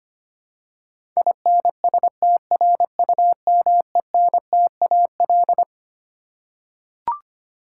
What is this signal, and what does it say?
Morse code, heard as a keyed tone, about 25 wpm, 700 Hz. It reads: INSTRUMENTAL
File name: Twelve Letter Words - 25wpm